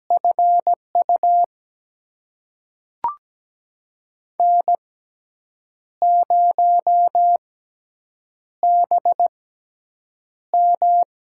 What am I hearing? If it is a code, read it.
Morse code, heard as a keyed tone, about 17 wpm, spaced as for 7 wpm, 700 Hz. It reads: FU N0BM
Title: USA Call Signs - 6x Character Spacing - 17wpm